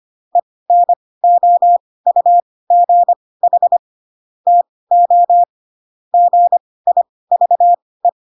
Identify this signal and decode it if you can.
Morse code, heard as a keyed tone, about 25 wpm, spaced as for 18 wpm, 700 Hz. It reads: ENOUGH TO GIVE
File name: Pinocchio / Pinocchio ch6 12 25 ik1hge